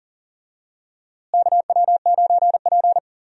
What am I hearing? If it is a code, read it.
Morse code, heard as a keyed tone, about 39 wpm, 700 Hz. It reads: KW9P
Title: USA Call Signs - 4 Characters - Repeated 3x - 40wpm